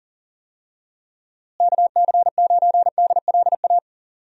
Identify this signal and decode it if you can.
Morse code, heard as a keyed tone, about 38 wpm, 700 Hz. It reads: KC9DLA
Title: USA Call Signs - 6 Characters - Repeated 3x - 40wpm